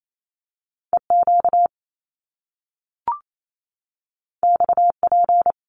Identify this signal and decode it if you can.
Morse code, heard as a keyed tone, about 28 wpm, 700 Hz. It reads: EQ XP
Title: Top Two Letter Combinations in Words 28wpm